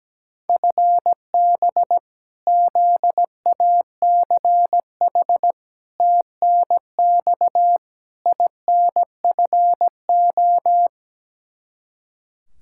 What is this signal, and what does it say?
Morse code, heard as a keyed tone, about 17 wpm, 700 Hz. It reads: FB ZACH TNX INFO
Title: K1USN SST QSO Practice 17wpm